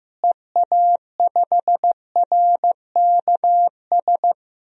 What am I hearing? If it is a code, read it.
Morse code, heard as a keyed tone, about 15 wpm, 700 Hz. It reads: EA5RKS
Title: DXCC-100 Call Signs - 6 Characters - Repeated 2x - 15wpm